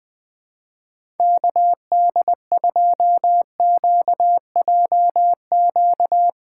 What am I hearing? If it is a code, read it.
Morse code, heard as a keyed tone, about 21 wpm, 700 Hz. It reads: KD2QJQ
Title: USA Callsign DE Callsign 20wpm